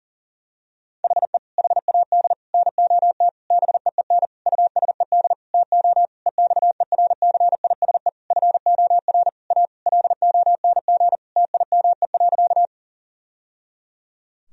Morse code, heard as a keyed tone, about 40 wpm, 700 Hz. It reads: HE HAD NOT BEEN USED TO EXERCISE FOR A LONG TIME.
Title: Sentences from Top 1000 Words - Review Part 4 - 40wpm